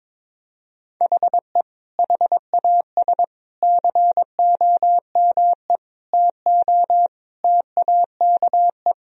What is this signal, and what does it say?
Morse code, heard as a keyed tone, about 21 wpm, 700 Hz. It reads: HE HAS COME TO TAKE
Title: Sentences from Top 1000 Words - Review Part 1 - 22wpm